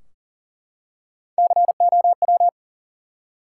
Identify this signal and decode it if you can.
Morse code, heard as a keyed tone, about 40 wpm, 700 Hz. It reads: COW
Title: Three Letter Words - 40wpm